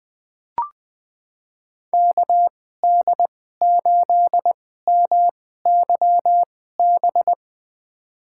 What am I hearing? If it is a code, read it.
Morse code, heard as a keyed tone, about 20 wpm, spaced as for 14 wpm, 700 Hz. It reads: KD8MYB
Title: USA Call Signs - 6 Characters - 2x Character Spacing - 20wpm